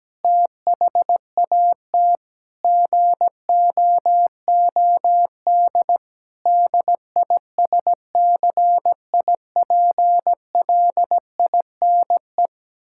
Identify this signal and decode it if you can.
Morse code, heard as a keyed tone, about 17 wpm, 700 Hz. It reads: THAT GOOD DISCIPLINE
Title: Sentences from Top 1000 Words - Review Part 6 - 17wpm